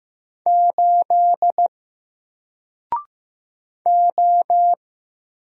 Morse code, heard as a keyed tone, about 15 wpm, 700 Hz. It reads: 8 O
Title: Single Letter-Number 15wpm